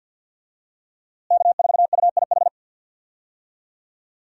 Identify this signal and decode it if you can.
Morse code, heard as a keyed tone, about 53 wpm, 700 Hz. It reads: K4UIH
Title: USA Call Signs - 5 Characters - Repeated 2x - 50wpm